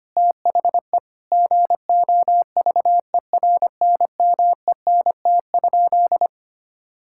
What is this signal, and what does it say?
Morse code, heard as a keyed tone, about 25 wpm, 700 Hz. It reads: THE GOVERNMENT?
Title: Sentences from Top 1600 Words 25wpm